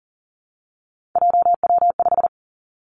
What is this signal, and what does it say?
Morse code, heard as a keyed tone, about 38 wpm, 700 Hz. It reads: JW5
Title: Three Challenging Letter-Number Combinations 40wpm